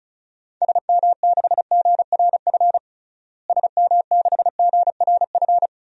Morse code, heard as a keyed tone, about 34 wpm, 700 Hz. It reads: SM6GRF SM6GRF
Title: DXCC-100 Call Signs - 6 Characters - Repeated 2x - 35wpm